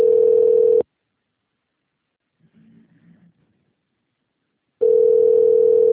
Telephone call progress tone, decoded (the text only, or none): ringback tone